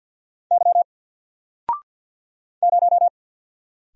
Morse code, heard as a keyed tone, about 48 wpm, 700 Hz. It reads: Y 0
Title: Single Letter-Number Alt 50wpm